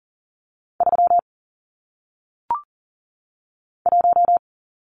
Morse code, heard as a keyed tone, about 41 wpm, 700 Hz. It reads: 3 1
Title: Numbers - 1-Digit 40wpm